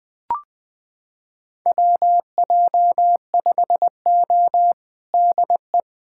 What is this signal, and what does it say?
Morse code, heard as a keyed tone, about 21 wpm, 700 Hz. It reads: WJ5O DE
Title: USA Callsign DE Callsign - 4 Characters - 20wpm